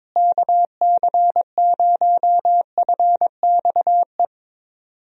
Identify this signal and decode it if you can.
Morse code, heard as a keyed tone, about 22 wpm, 700 Hz. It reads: KC0FXE
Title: USA Call Signs - 6 Characters - Repeated 2x - 22wpm